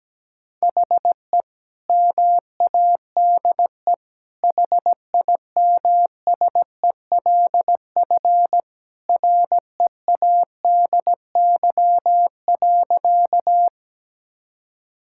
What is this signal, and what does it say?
Morse code, heard as a keyed tone, about 17 wpm, 700 Hz. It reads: HE MADE HIMSELF READY.
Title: Sentences from Top 1600 Words 17wpm